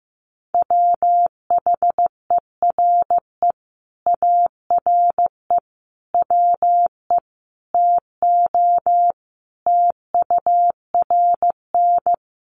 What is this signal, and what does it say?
Morse code, heard as a keyed tone, about 15 wpm, 700 Hz. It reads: WHERE ARE WE TO TURN
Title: Sentences from Top 200 Words 15wpm